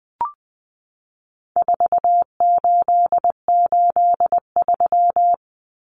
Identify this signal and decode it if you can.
Morse code, heard as a keyed tone, about 19 wpm, 700 Hz. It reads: 4883
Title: Numbers - 4-Digit 20wpm